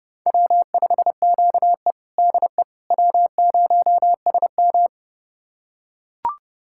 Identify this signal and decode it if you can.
Morse code, heard as a keyed tone, about 30 wpm, 700 Hz. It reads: W5QE DE W0SM
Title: USA Callsign DE Callsign - 4 Characters - 30wpm